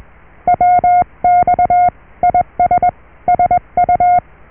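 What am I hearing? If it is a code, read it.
Morse code, heard as a keyed tone, about 21 wpm, 720 Hz. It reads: WX IS SU
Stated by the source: Morse QSO on 14 MHz with a German station, received at RST 579.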